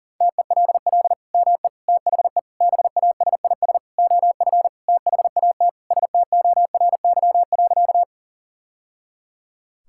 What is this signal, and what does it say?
Morse code, heard as a keyed tone, about 37 wpm, 700 Hz. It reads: TELL ME THE BASIS OF THAT STORY.
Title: Sentences from Top 1000 Words - Review Part 5 - 40wpm